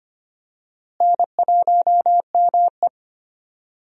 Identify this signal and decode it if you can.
Morse code, heard as a keyed tone, about 25 wpm, 700 Hz. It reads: N1ME
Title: USA Call Signs - 4 Characters - Repeated 3x - 25wpm